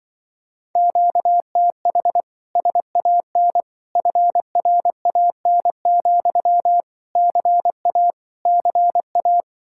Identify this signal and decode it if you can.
Morse code, heard as a keyed tone, about 23 wpm, 700 Hz. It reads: QTH SAN FRAN, CA CA
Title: Speed-Racing Common QSO Phrases - 20wpm